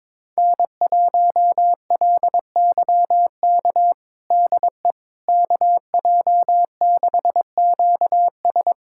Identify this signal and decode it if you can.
Morse code, heard as a keyed tone, about 22 wpm, 700 Hz. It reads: N1LYK DE KJ6QS
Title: USA Callsign DE Callsign - 5 Characters - 22wpm